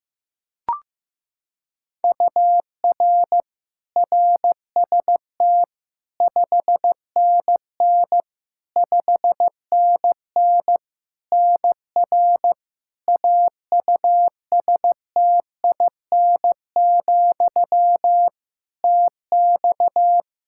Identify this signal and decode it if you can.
Morse code, heard as a keyed tone, about 15 wpm, 700 Hz. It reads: UR RST 5NN 5NN NR AUSTIN, TX
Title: Practice QSO - Exchange 1 - Beginning - 15wpm